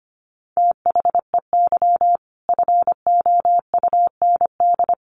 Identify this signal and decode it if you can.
Morse code, heard as a keyed tone, about 25 wpm, 700 Hz. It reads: THEY FOUND
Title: Sentences from Top 1500 Words 25wpm